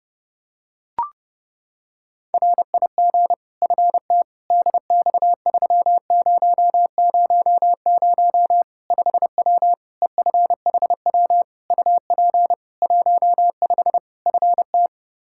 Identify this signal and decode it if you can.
Morse code, heard as a keyed tone, about 30 wpm, 700 Hz. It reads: RIG FT DX3000 5W EFHW UP 15 FT
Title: K1USN SST QSO Practice 30wpm